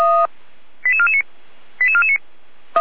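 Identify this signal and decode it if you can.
Five-tone selective call (ZVEI), heard as quick runs of five tones, tones of about 70 ms. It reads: 88449 88449
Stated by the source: ZVEI five-tone selective call received on an SDR, fire brigade alerting.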